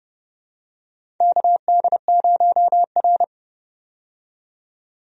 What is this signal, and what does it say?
Morse code, heard as a keyed tone, about 32 wpm, 700 Hz. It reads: KD0R
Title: USA Call Signs - 4 Characters - 30wpm